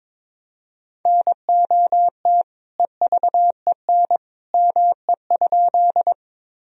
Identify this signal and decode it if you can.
Morse code, heard as a keyed tone, about 22 wpm, 700 Hz. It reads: NOT EVEN ME?
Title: Sentences from Top 1000 Words - Review Part 1 - 22wpm